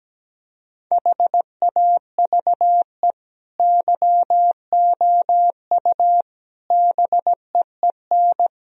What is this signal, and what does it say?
Morse code, heard as a keyed tone, about 17 wpm, 700 Hz. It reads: HAVE YOU BEEN
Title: Sentences from Top 1000 Words - Review Part 4 - 17wpm